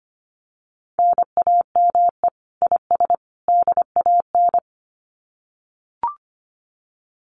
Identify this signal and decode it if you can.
Morse code, heard as a keyed tone, about 25 wpm, 700 Hz. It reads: NAME IS DAN
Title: Common QSO Phrases 25wpm